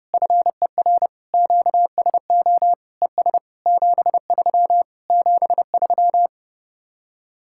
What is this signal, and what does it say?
Morse code, heard as a keyed tone, about 33 wpm, 700 Hz. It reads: FER QSO ES 73 73